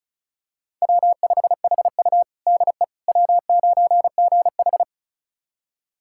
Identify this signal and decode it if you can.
Morse code, heard as a keyed tone, about 36 wpm, 700 Hz. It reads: W5HU DE W9GH
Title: USA Callsign DE Callsign - 4 Characters - 35wpm